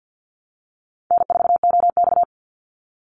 Morse code, heard as a keyed tone, about 50 wpm, 700 Hz. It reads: N4OX
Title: Easy Call Signs - 50wpm